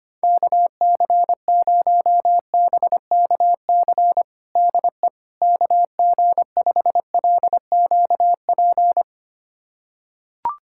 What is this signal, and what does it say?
Morse code, heard as a keyed tone, about 25 wpm, 700 Hz. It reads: KC0BKC DE KG5LQP
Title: USA Callsign DE Callsign - 6 Characters - 25wpm